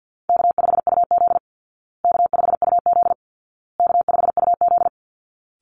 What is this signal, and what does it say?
Morse code, heard as a keyed tone, about 50 wpm, 700 Hz. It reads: K5UZ K5UZ K5UZ
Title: Easy Call Signs Repeated 3x - 50wpm